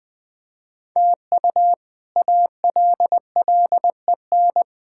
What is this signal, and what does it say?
Morse code, heard as a keyed tone, about 20 wpm, 700 Hz. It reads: TU ALLEN
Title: K1USN SST QSO Practice - 20wpm